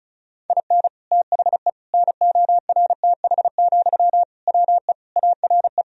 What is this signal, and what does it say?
Morse code, heard as a keyed tone, about 34 wpm, 700 Hz. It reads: IN THE NORTH, WE ARE
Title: Sentences from Top 1000 Words - Review Part 1 - 35wpm